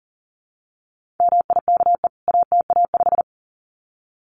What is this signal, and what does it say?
Morse code, heard as a keyed tone, about 37 wpm, 700 Hz. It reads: MIKE ATA5
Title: CWT Exchange 40wpm